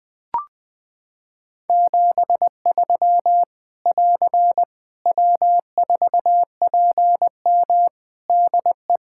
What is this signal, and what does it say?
Morse code, heard as a keyed tone, about 20 wpm, 700 Hz. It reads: 73 <AR> W4PM DE
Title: K1USN SST QSO Practice 20wpm